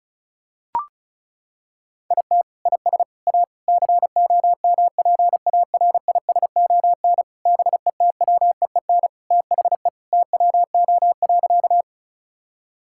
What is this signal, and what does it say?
Morse code, heard as a keyed tone, about 34 wpm, 700 Hz. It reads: IT IS A COMPARISON BETWEEN THE TWO.
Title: Sentences from Top 1600 Words 35wpm